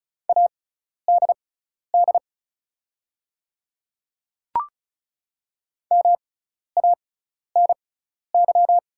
Morse code, heard as a keyed tone, about 36 wpm, spaced as for 12 wpm, 690 Hz. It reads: ADD MANY
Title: 6x Character Spacing - Top 300 Words - 35wpm